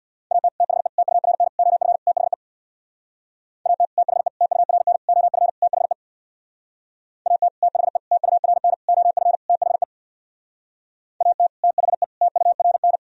Morse code, heard as a keyed tone, about 61 wpm, 700 Hz. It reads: AT THE TURN OF THE AT THE TURN OF THE AT THE TURN OF THE AT THE TURN
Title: Speed-Racing - Sets of 5 Words 50wpm